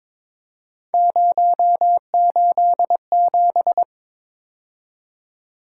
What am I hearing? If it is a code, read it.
Morse code, heard as a keyed tone, about 22 wpm, 700 Hz. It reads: 087
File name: Three Letter-Number Combinations - 22wpm